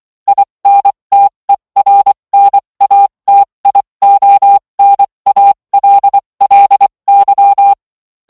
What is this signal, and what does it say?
Morse code, heard as a keyed tone, about 24 wpm, spaced as for 20 wpm, 800 Hz. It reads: INTERNATIONALLY